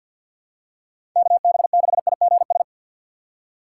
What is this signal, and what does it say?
Morse code, heard as a keyed tone, about 53 wpm, 700 Hz. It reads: KB6IGS
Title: USA Call Signs - 2x Character Spacing - 50wpm